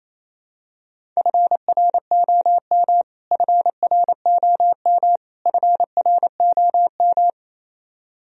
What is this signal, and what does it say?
Morse code, heard as a keyed tone, about 28 wpm, 700 Hz. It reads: FROM FROM FROM
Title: Top 100 Words Repeated 28wpm